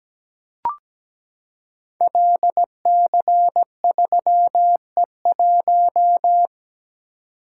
Morse code, heard as a keyed tone, about 17 wpm, 700 Hz. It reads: LC3E1
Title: Code Groups - 5 Letter:Numbers - 17wpm